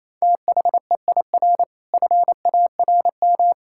Morse code, heard as a keyed tone, about 28 wpm, 700 Hz. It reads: THEIR FARM
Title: Sentences from Top 1000 Words - Review Part 6 - 28wpm